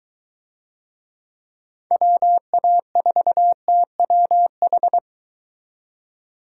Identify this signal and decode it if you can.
Morse code, heard as a keyed tone, about 23 wpm, 700 Hz. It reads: WA4TWH